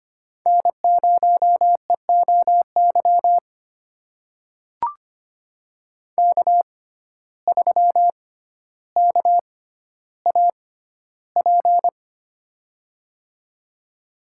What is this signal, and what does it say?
Morse code, heard as a keyed tone, about 25 wpm, spaced as for 13 wpm, 700 Hz. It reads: N0EOY K3KAP
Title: USA Call Signs - 5 Characters - 6x Character Spacing - 25wpm